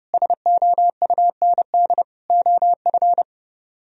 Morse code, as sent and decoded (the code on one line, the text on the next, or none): ... --- ..- -. -.. / --- ..-.
SOUND OF